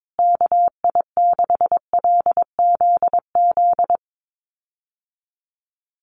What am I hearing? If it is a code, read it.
Morse code, heard as a keyed tone, about 22 wpm, 700 Hz. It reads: KI6LZZ
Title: USA Call Signs Repeated 3x 22wpm